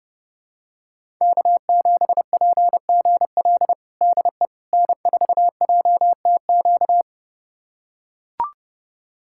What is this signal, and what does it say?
Morse code, heard as a keyed tone, about 30 wpm, 700 Hz. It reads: K7PGL DE N4JTQ
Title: USA Callsign DE Callsign - 5 Characters - 30wpm